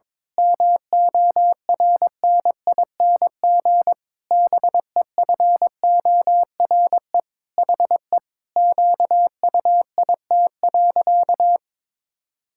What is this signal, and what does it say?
Morse code, heard as a keyed tone, about 22 wpm, 700 Hz. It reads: MORNING BEFORE HE QUIT.
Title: Pangram Challenge v2 - 22wpm